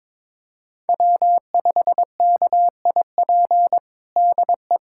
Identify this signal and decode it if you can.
Morse code, heard as a keyed tone, about 22 wpm, 700 Hz. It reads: W5KIP DE
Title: USA Callsign DE Callsign - 5 Characters - 22wpm